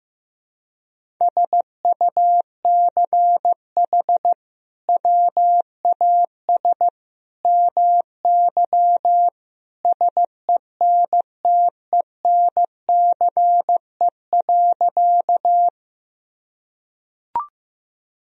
Morse code, heard as a keyed tone, about 15 wpm, 700 Hz. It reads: SUCH WAS MY SENTENCE.
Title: Sentences from Top 1000 Words - Review Part 1 - 15wpm